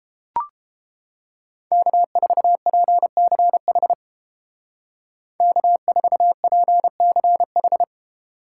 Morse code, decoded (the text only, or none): K4PCH K4PCH